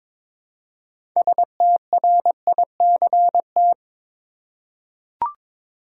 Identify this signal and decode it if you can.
Morse code, heard as a keyed tone, about 22 wpm, 700 Hz. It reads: STRICT
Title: Six Letter Words - 22wpm